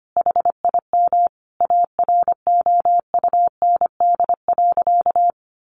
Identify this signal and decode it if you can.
Morse code, heard as a keyed tone, about 25 wpm, 700 Hz. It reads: HIM AROUND.
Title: Easy Phrases 25wpm